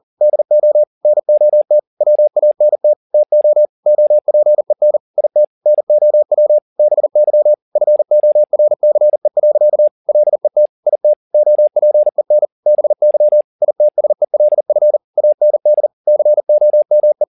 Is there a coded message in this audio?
Yes: Morse code from 0.2 to 17.2 s, heard as a keyed tone, about 40 wpm, 600 Hz. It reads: DO NOT WANT TO OPEN IT NOW BY FORCE. LET IT OPEN BY ITSELF AND COME